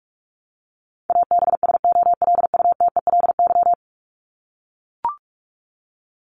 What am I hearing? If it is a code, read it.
Morse code, heard as a keyed tone, about 44 wpm, 700 Hz. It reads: ABSOLUTELY
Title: LY Suffix - Top 100 Words with Base Words 45wpm